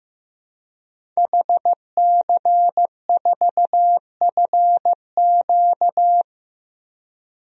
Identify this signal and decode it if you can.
Morse code, heard as a keyed tone, about 15 wpm, 700 Hz. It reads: HC4FQ
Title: DXCC-100 Call Signs - 5 Characters - 15wpm